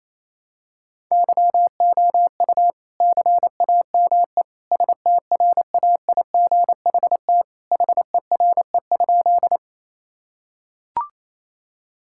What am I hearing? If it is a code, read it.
Morse code, heard as a keyed tone, about 28 wpm, 700 Hz. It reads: YOU CAME STRAIGHT HERE?
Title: Sentences from Top 1000 Words - Review Part 6 - 28wpm